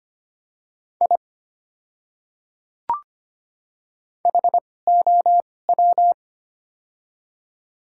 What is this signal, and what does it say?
Morse code, heard as a keyed tone, about 25 wpm, spaced as for 18 wpm, 700 Hz. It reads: I HOW
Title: 2x Character Spacing - Top 100 Words - 25wpm